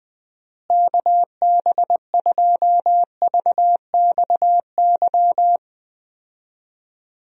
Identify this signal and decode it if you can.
Morse code, heard as a keyed tone, about 20 wpm, 700 Hz. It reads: KB2VXY
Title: USA Call Signs - 6 Characters - Repeated 3x - 20wpm